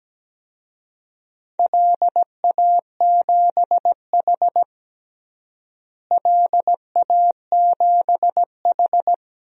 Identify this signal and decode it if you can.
Morse code, heard as a keyed tone, about 17 wpm, 700 Hz. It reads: LA7H LA7H
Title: DXCC-100 Call Signs - 4 Characters - Repeated 3x - 17wpm